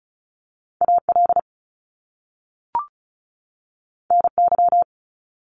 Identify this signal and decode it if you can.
Morse code, heard as a keyed tone, about 36 wpm, 700 Hz. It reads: AL NY
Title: Top Two Letter Combinations in Words 35wpm